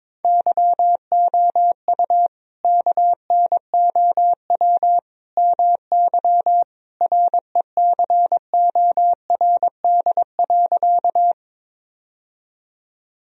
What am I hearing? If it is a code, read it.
Morse code, heard as a keyed tone, about 22 wpm, 700 Hz. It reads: YOU KNOW MY RECORD.
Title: Sentences from Top 1000 Words - Review Part 3 - 22wpm